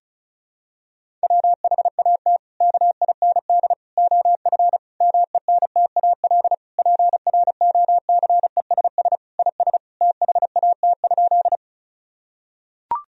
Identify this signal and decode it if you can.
Morse code, heard as a keyed tone, about 34 wpm, 700 Hz. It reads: WHAT KIND OF MENTAL PROCESS IS THAT?
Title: Sentences from Top 1000 Words - Review Part 4 - 35wpm